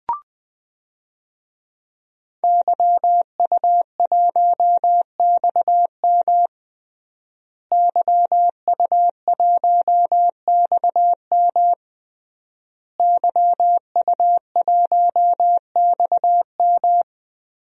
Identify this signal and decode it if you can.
Morse code, heard as a keyed tone, about 20 wpm, 700 Hz. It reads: YU1XM YU1XM YU1XM
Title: DXCC-100 Call Signs - 5 Characters - Repeated 3x - 20wpm